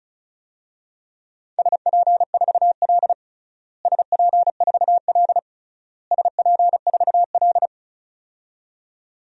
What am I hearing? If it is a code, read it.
Morse code, heard as a keyed tone, about 36 wpm, 700 Hz. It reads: SP4L SP4L SP4L
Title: DXCC-100 Call Signs - 4 Characters - Repeated 3x - 35wpm